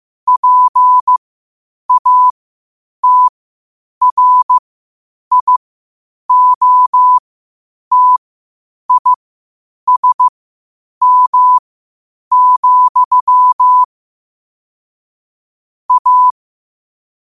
Morse code, heard as a keyed tone, about 15 wpm, spaced as for 8 wpm, 1000 Hz. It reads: PATRIOTISM, A
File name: qotdfiles / 8wpm